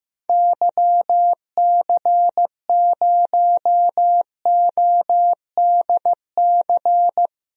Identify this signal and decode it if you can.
Morse code, heard as a keyed tone, about 15 wpm, 700 Hz. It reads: YC0ODC
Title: DXCC-100 Call Signs - 6 Characters - 15wpm